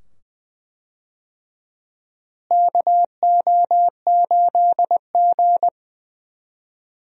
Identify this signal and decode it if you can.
Morse code, heard as a keyed tone, about 20 wpm, 700 Hz. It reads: KO8G
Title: USA Call Signs - 4 Characters - Repeated 2x - 20wpm